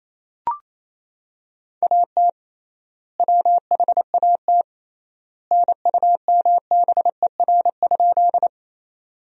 Morse code, heard as a keyed tone, about 28 wpm, 700 Hz. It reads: AT WHAT NUMBER?
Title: 3x Word-Spacing - Sentences from Top 100 Words - 28wpm